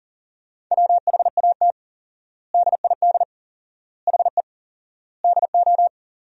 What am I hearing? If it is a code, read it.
Morse code, heard as a keyed tone, about 40 wpm, 700 Hz. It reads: WHAT DID HE DO
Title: 4x Word-Spacing - Sentences from Top 100 Words - 40wpm